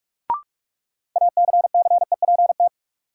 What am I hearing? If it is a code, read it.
Morse code, heard as a keyed tone, about 41 wpm, 700 Hz. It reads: ACCEPT
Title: Sentences from Top 1300 Words - New Word Drill - 45wpm